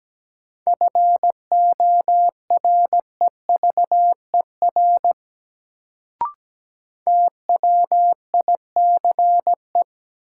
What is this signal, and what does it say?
Morse code, heard as a keyed tone, about 17 wpm, 700 Hz. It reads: FOREVER TWICE
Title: Sentences from Top 1600 Words - New Word Drill - 17wpm